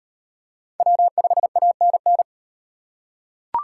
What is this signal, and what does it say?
Morse code, heard as a keyed tone, about 37 wpm, 700 Hz. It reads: W5ANN